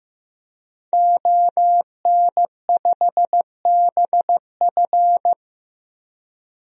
Morse code, heard as a keyed tone, about 15 wpm, 700 Hz. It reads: ON5BF
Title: DXCC-100 Call Signs - 5 Characters - 15wpm